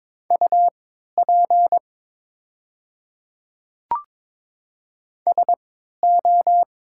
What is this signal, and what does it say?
Morse code, heard as a keyed tone, about 22 wpm, spaced as for 12 wpm, 700 Hz. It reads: UP SO
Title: 3x Character Spacing - Top 100 Words - 22wpm